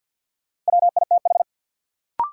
Morse code, heard as a keyed tone, about 53 wpm, 700 Hz. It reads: WITH
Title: Sets of 2 Words - New Word Drill - 50wpm